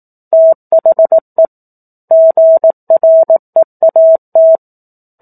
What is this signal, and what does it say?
Morse code, heard as a keyed tone, about 18 wpm, 650 Hz. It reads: THE GREAT
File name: 18 WPM